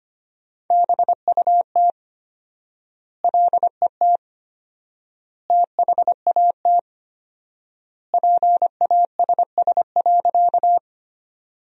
Morse code, heard as a keyed tone, about 25 wpm, 700 Hz. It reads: BUT LET THAT PASS.
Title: 4x Word-Spacing - Sentences from Top 400 Words - 25wpm